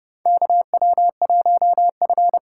Morse code, heard as a keyed tone, about 30 wpm, 700 Hz. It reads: KW1F
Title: USA Call Signs - 4 Characters - 4x Character Spacing - 30wpm